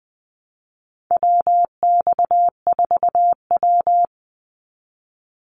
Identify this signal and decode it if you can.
Morse code, heard as a keyed tone, about 20 wpm, 700 Hz. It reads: WX4W
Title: Call Signs Repeated 2x - 20wpm